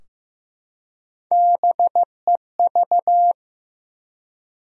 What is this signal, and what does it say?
Morse code, heard as a keyed tone, about 15 wpm, 700 Hz. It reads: BEV
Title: USA Common Names - Max 3 Letters - 15wpm